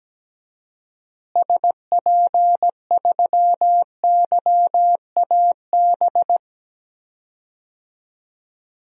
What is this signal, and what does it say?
Morse code, heard as a keyed tone, about 17 wpm, 700 Hz. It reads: SP3YAB
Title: DXCC-100 Call Signs - 6 Characters - Repeated 2x - 17wpm